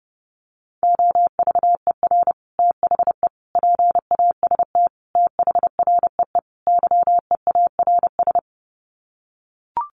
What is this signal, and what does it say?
Morse code, heard as a keyed tone, about 30 wpm, 700 Hz. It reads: OVER THE PAST THREE YEARS
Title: Sets of 5 Words Encore2 30wpm